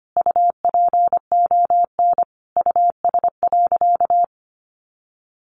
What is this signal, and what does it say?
Morse code, heard as a keyed tone, about 25 wpm, 700 Hz. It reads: UPON US.
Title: Sentences from Top 1500 Words 25wpm